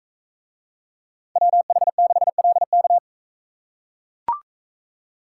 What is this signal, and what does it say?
Morse code, heard as a keyed tone, about 40 wpm, 700 Hz. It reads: WH6LK